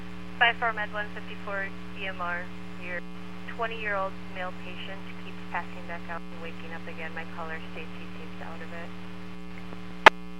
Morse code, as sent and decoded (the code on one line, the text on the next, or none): none